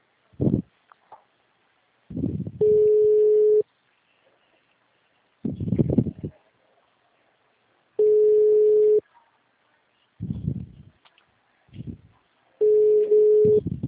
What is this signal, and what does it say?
Ringback tone (Europe, 425 Hz).